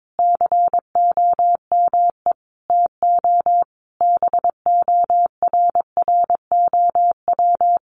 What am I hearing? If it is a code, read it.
Morse code, heard as a keyed tone, about 22 wpm, 700 Hz. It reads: COME TO BORROW
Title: Sentences from Top 1500 Words 22wpm